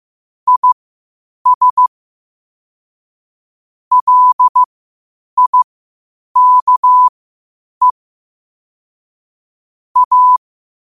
Morse code, heard as a keyed tone, about 15 wpm, spaced as for 8 wpm, 1000 Hz. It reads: IS LIKE A
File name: qotdfiles / 8wpm